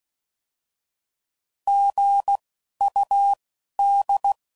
Morse code, heard as a keyed tone, about 16 wpm, spaced as for 11 wpm, 800 Hz. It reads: GUD
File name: Mixed / Misc / Words